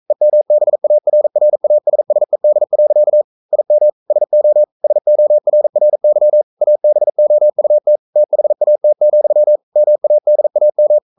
Morse code, heard as a keyed tone, about 42 wpm, 600 Hz. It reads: EMBARRASSED. IM SO SORRY ABOUT THAT, MADAM
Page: Romanen Never af Ken Follet på engelsk.